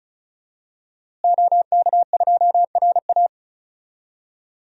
Morse code, heard as a keyed tone, about 36 wpm, 700 Hz. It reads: OK2RA
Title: DXCC-100 Call Signs - 5 Characters - Repeated 2x - 35wpm